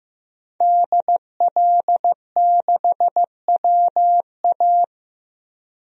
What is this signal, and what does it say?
Morse code, heard as a keyed tone, about 15 wpm, 700 Hz. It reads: DL6WA
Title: DXCC-100 Call Signs - 5 Characters - 15wpm